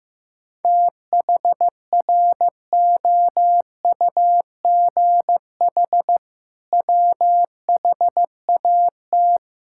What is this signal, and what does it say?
Morse code, heard as a keyed tone, about 14 wpm, 700 Hz. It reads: THROUGH WHAT
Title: Sentences from Top 1000 Words - Review Part 6 - 15wpm